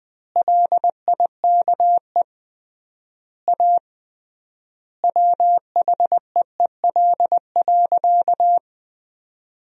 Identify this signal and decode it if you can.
Morse code, heard as a keyed tone, about 20 wpm, 700 Hz. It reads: LIKE A WHEEL.
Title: 3x Word-Spacing - Sentences from Top 500 Words - 20wpm